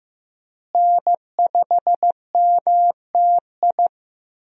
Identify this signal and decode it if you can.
Morse code, heard as a keyed tone, about 15 wpm, 700 Hz. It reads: N5MTI
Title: USA Call Signs - 5 Characters - Repeated 3x - 15wpm